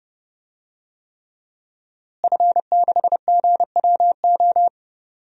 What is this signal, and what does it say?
Morse code, heard as a keyed tone, about 32 wpm, 700 Hz. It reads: F6GWO
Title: DXCC-100 Call Signs - 5 Characters - Repeated 3x - 30wpm